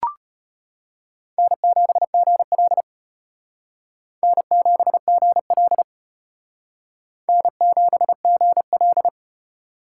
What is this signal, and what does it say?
Morse code, heard as a keyed tone, about 32 wpm, 700 Hz. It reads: N7GL N7GL N7GL